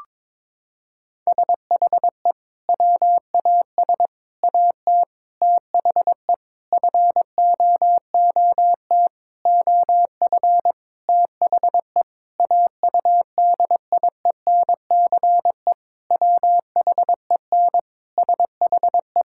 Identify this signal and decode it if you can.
Morse code, heard as a keyed tone, about 22 wpm, 700 Hz. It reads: SHE WAS AT THE FOOT OF THE AUDIENCE WHEN SHE
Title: Sentences from Top 1000 Words - Review Part 5 - 22wpm